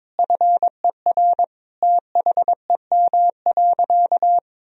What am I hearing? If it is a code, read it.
Morse code, heard as a keyed tone, about 22 wpm, 700 Hz. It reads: FER THEM.
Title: Sentences from Top 300 Words and Abbreviations 22wpm